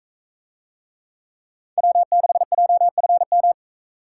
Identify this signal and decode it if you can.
Morse code, heard as a keyed tone, about 44 wpm, 700 Hz. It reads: W6JFM